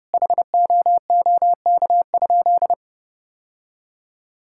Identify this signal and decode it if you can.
Morse code, heard as a keyed tone, about 30 wpm, 700 Hz. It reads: HOOK?
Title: Sentences from Top 1600 Words 30wpm